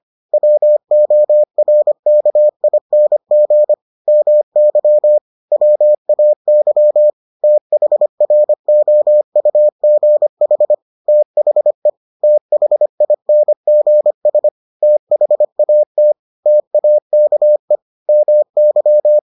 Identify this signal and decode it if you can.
Morse code, heard as a keyed tone, about 25 wpm, 600 Hz. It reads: WORKING MY WAY THROUGH THE THINGS THAT TAKE MY